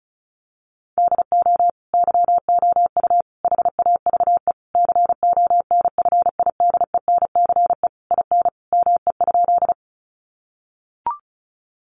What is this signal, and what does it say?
Morse code, heard as a keyed tone, about 34 wpm, 700 Hz. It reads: DO YOU HAVE CONFIDENCE IN ME?
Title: Sentences from Top 1500 Words 35wpm